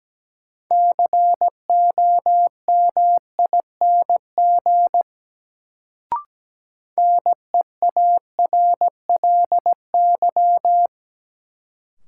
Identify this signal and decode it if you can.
Morse code, heard as a keyed tone, about 17 wpm, 700 Hz. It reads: COMING NEARLY
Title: Sentences from Top 1300 Words - New Word Drill - 17wpm